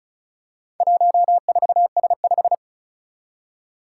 Morse code, heard as a keyed tone, about 36 wpm, 700 Hz. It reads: 14S5
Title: Four Challenging Letter-Number Combinations 35wpm